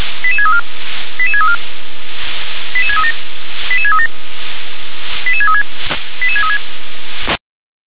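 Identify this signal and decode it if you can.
Five-tone selective call (ZVEI), heard as quick runs of five tones, tones of about 70 ms. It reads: 88634 88634 88637 88637 88636 88636
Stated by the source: ZVEI five-tone call 88631 received on SDR.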